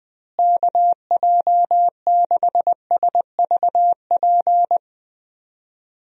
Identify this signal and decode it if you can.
Morse code, heard as a keyed tone, about 20 wpm, 700 Hz. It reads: KJ6SVP
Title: USA Call Signs - 6 Characters - Repeated 2x - 20wpm